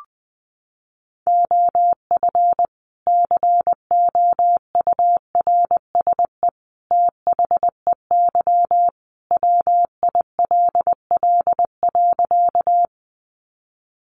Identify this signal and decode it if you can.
Morse code, heard as a keyed tone, about 20 wpm, 700 Hz. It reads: OF COURSE THEY WILL.
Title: Sentences from Top 500 Words 20wpm